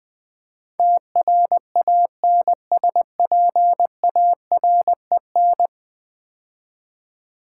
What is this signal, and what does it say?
Morse code, heard as a keyed tone, about 20 wpm, 700 Hz. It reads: TRANSPAREN
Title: Build A Word - 5 Letters At A Time - 12-Letter Words - 20wpm